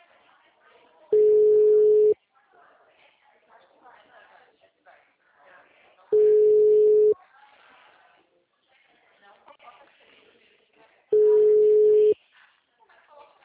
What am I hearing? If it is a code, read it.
Ringback tone (Europe, 425 Hz), with speech.
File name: out- -2012- - - .24156